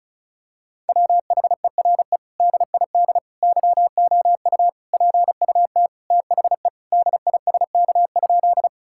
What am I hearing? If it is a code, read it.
Morse code, heard as a keyed tone, about 36 wpm, 700 Hz. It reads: WHERE DID YOU PUT THE DISK?
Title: Sentences from Top 1000 Words - Review Part 6 - 35wpm